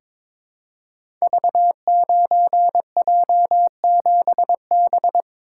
Morse code, heard as a keyed tone, about 22 wpm, 700 Hz. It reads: V9J7B
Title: Code Groups - 5 Letter:Numbers - 22wpm